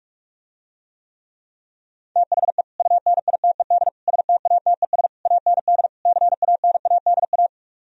Morse code, heard as a keyed tone, about 49 wpm, 700 Hz. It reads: THE UNITED STATES AND CANADA